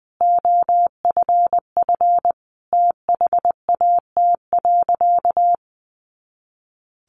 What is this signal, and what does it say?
Morse code, heard as a keyed tone, about 20 wpm, 700 Hz. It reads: OFF THAT.
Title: Sentences from Top 400 Words 20wpm